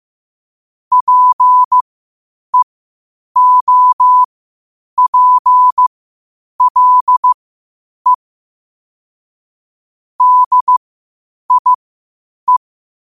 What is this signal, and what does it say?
Morse code, heard as a keyed tone, about 15 wpm, spaced as for 8 wpm, 1000 Hz. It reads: PEOPLE DIE
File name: qotdfiles / 8wpm